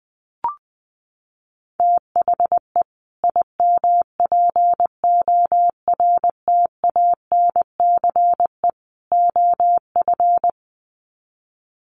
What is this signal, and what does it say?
Morse code, heard as a keyed tone, about 20 wpm, 700 Hz. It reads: THE IMPORTANCE OF
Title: Sets of 3 Words 20wpm